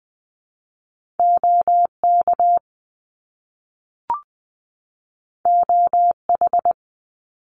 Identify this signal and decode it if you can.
Morse code, heard as a keyed tone, about 20 wpm, 700 Hz. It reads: OK OH
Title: Top Two Letter Combinations in Words 20wpm